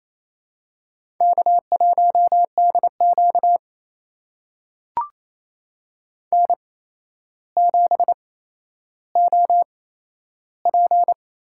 Morse code, heard as a keyed tone, about 28 wpm, spaced as for 12 wpm, 700 Hz. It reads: K1DQ N7OP
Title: USA Call Signs - 4 Characters - 8x Character Spacing - 28wpm